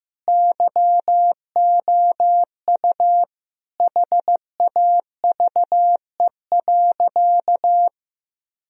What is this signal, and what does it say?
Morse code, heard as a keyed tone, about 15 wpm, 700 Hz. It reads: YOU HAVE.
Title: Sentences from Top 1000 Words - Review Part 4 - 15wpm